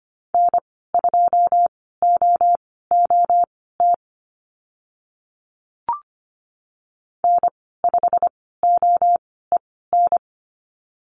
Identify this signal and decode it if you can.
Morse code, heard as a keyed tone, about 25 wpm, spaced as for 16 wpm, 690 Hz. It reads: N2OOT N5OEN
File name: Lesson 15g - Number 5 - Call Signs 25wpm